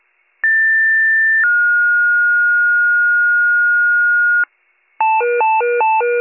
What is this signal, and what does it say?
Two-tone page, heard as a one-second tone, then a three-second tone, about 1800 then 1430 Hz, with noise. It is heard from 0.4 to 4.4 s.